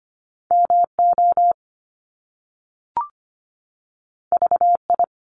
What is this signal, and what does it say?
Morse code, heard as a keyed tone, about 25 wpm, 700 Hz. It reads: MO VI
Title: ARRL Sections 25wpm